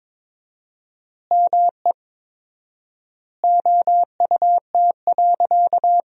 Morse code, heard as a keyed tone, about 22 wpm, 700 Hz. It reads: ME OUT.
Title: 4x Word-Spacing - Sentences from Top 500 Words - 22wpm